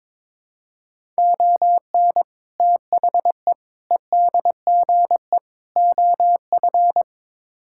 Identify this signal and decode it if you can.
Morse code, heard as a keyed tone, about 22 wpm, 700 Hz. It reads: ON THE EDGE OF
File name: practice / 22wpm / Build A Phrase - 5-Word Phrases - 22wpm